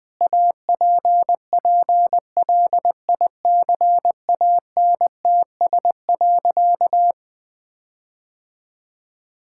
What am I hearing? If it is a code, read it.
Morse code, heard as a keyed tone, about 20 wpm, 700 Hz. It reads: APPLICANTS.
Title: Sentences with 6-Word Phrases - 20wpm